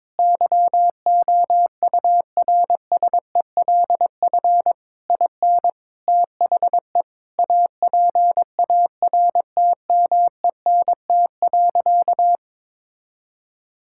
Morse code, heard as a keyed tone, about 22 wpm, 700 Hz. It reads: YOURSELF IN THE APARTMENT.
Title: Sentences from Top 1000 Words - Review Part 6 - 22wpm